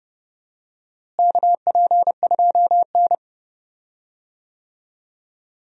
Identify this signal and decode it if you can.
Morse code, heard as a keyed tone, about 30 wpm, 700 Hz. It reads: KP2N
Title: DXCC-100 Call Signs - 4 Characters - Repeated 3x - 30wpm